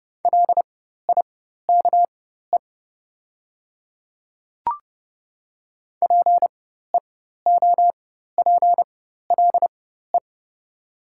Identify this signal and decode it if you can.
Morse code, heard as a keyed tone, about 30 wpm, spaced as for 14 wpm, 700 Hz. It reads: LIKE PEOPLE
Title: 4x Character Spacing - Top 100 Words - 30wpm